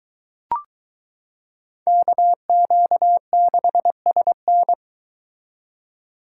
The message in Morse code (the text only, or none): KQ6SN